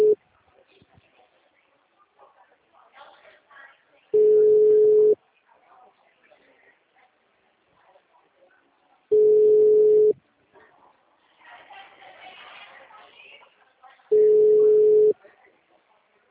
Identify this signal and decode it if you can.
Ringback tone (Europe, 425 Hz), with speech.
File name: out- -2006- - - .76713